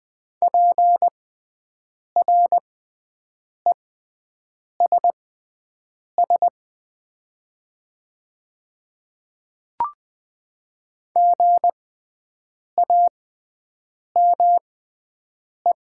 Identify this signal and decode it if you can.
Morse code, heard as a keyed tone, about 20 wpm, spaced as for 7 wpm, 700 Hz. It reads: PRESS GAME
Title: 6x Character Spacing - Top 500 Words - 20wpm